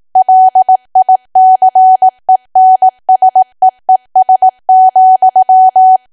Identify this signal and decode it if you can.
Morse code, heard as a keyed tone, about 18 wpm, 750 Hz. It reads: LICENSEES,
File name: cwa / W1AW-Level2 / 18WPM